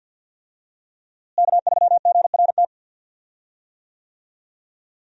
Morse code, heard as a keyed tone, about 50 wpm, 700 Hz. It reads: K3GRT
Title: USA Call Signs - 5 Characters - Repeated 3x - 50wpm